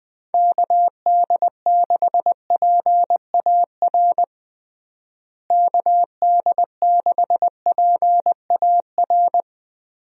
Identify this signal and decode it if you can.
Morse code, heard as a keyed tone, about 20 wpm, 700 Hz. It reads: KD6PAR KD6PAR
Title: USA Call Signs - 6 Characters - Repeated 3x - 20wpm